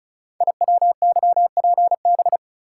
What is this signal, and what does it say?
Morse code, heard as a keyed tone, about 36 wpm, 700 Hz. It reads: IWYPB
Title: Code Groups - 5 Letters - 35wpm